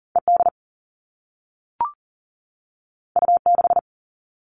Morse code, heard as a keyed tone, about 40 wpm, 700 Hz. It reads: ED U6
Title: All Cut Numbers - 2-Digit 40wpm